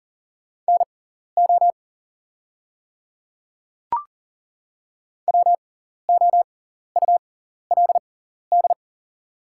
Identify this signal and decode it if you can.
Morse code, heard as a keyed tone, about 41 wpm, spaced as for 14 wpm, 690 Hz. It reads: NO WOULD
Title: 6x Character Spacing - Top 300 Words - 40wpm